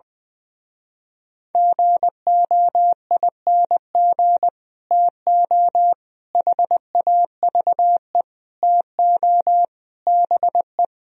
Morse code, heard as a keyed tone, about 21 wpm, 700 Hz. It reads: GOING TO HAVE TO BE
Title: Speed-Racing - Sets of 5 Words 20wpm